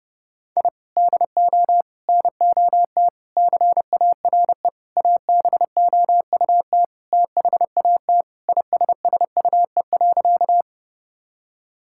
Morse code, heard as a keyed tone, about 32 wpm, 700 Hz. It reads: I DO NOT CARE ABOUT THAT ISSUE.
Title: Sentences from Top 1000 Words - Review Part 6 - 30wpm